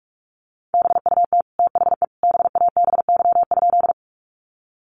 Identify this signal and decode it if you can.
Morse code, heard as a keyed tone, about 44 wpm, 700 Hz. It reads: BUT THE BABY?
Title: Sentences from Top 1300 Words 45wpm